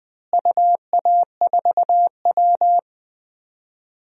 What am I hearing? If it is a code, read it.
Morse code, heard as a keyed tone, about 21 wpm, 690 Hz. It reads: UA4W
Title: DXCC-100 Call Signs - 4 Characters - Repeated 3x - 20wpm